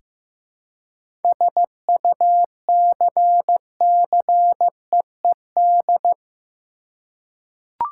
Morse code, heard as a keyed tone, about 15 wpm, 700 Hz. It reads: SUCCEED
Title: Seven Letter Words - 15wpm